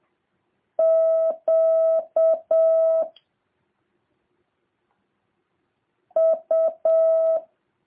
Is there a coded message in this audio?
Yes: Morse code from 0.8 to 7.4 s, heard as a keyed tone, about 7 wpm, 650 Hz. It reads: Q U